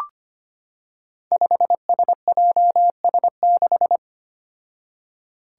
Morse code, heard as a keyed tone, about 25 wpm, 700 Hz. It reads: 5SJS6
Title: Five Challenging Letter-Number Combinations 25wpm